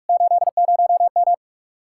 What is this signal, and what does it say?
Morse code, heard as a keyed tone, about 49 wpm, 700 Hz. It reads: 80M